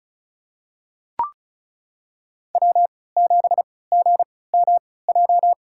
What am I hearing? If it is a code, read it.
Morse code, heard as a keyed tone, about 36 wpm, spaced as for 20 wpm, 700 Hz. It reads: W7GMJ
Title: USA Call Signs - 5 Characters - 3x Character Spacing - 35wpm